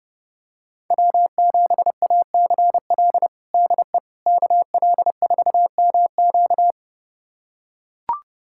Morse code, heard as a keyed tone, about 30 wpm, 700 Hz. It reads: W7ACL DE KL4MQ
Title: USA Callsign DE Callsign - 5 Characters - 30wpm